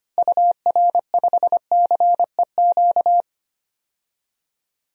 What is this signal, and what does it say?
Morse code, heard as a keyed tone, about 25 wpm, 700 Hz. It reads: UR5CEQ
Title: DXCC-100 Call Signs - 6 Characters - Repeated 2x - 25wpm